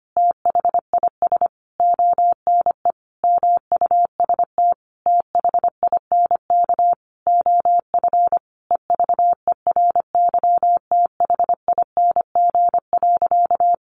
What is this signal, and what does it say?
Morse code, heard as a keyed tone, about 25 wpm, 700 Hz. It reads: THIS ONE MUST THINK OF EVERYTHING.
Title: Sentences from Top 1000 Words 25wpm